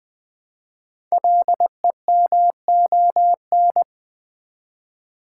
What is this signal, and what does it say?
Morse code, heard as a keyed tone, about 20 wpm, 700 Hz. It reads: LEMON
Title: Build A Word - 1 Letter At A Time - 5-Letter Words - 20wpm